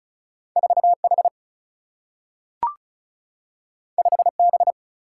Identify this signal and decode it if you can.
Morse code, heard as a keyed tone, about 36 wpm, 700 Hz. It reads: 4 H 5 B
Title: Single-Letter Single-Number 35wpm